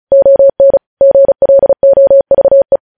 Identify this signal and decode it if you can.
Morse code, heard as a keyed tone, about 36 wpm, 550 Hz. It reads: ON GLOVE